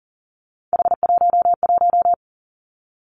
Morse code, heard as a keyed tone, about 42 wpm, 700 Hz. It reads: H11
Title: Three Challenging Letter-Number Combinations 40wpm